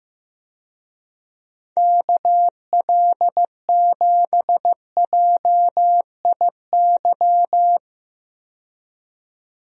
Morse code, heard as a keyed tone, about 15 wpm, 700 Hz. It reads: KL7JIY
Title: USA Call Signs - 6 Characters - 15wpm